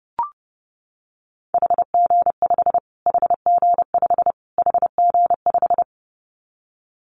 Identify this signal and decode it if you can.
Morse code, heard as a keyed tone, about 30 wpm, 700 Hz. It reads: HG5 HG5 HG5
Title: Callsign Prefixes for Top 25 DXCC - Repeated 3x - 30wpm